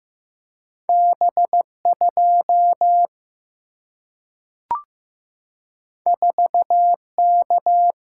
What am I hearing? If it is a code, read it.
Morse code, heard as a keyed tone, about 15 wpm, 700 Hz. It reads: B2 4K
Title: Single-Letter Single-Number 15wpm